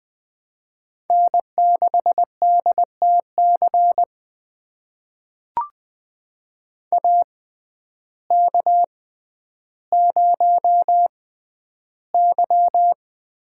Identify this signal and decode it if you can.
Morse code, heard as a keyed tone, about 20 wpm, spaced as for 11 wpm, 700 Hz. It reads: N6DTC AK0Y
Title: USA Call Signs - 6x Character Spacing - 20wpm